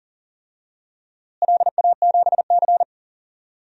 Morse code, heard as a keyed tone, about 40 wpm, 700 Hz. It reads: LA7C
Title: DXCC-100 Call Signs - 4 Characters - Repeated 2x - 40wpm